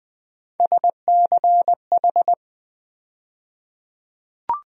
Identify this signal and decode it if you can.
Morse code, heard as a keyed tone, about 20 wpm, 700 Hz. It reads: SCH
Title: USA Call Sign Postfixes - 3 Letters 20wpm